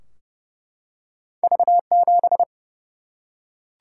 Morse code, heard as a keyed tone, about 29 wpm, 700 Hz. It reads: V7
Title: Single-Letter Single-Number 30wpm